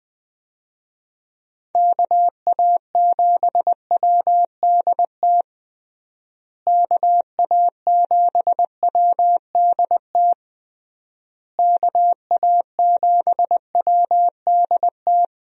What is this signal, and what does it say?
Morse code, heard as a keyed tone, about 20 wpm, 700 Hz. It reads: KA7WDT KA7WDT KA7WDT
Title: USA Call Signs - 6 Characters - Repeated 3x - 20wpm